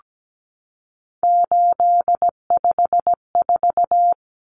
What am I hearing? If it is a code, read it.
Morse code, heard as a keyed tone, about 17 wpm, 700 Hz. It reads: 854
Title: Common Cut Numbers - 3-Digit 17wpm